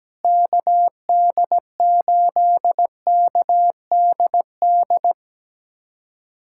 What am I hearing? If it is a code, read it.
Morse code, heard as a keyed tone, about 17 wpm, 700 Hz. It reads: KD8KDD
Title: USA Callsign DE Callsign 17wpm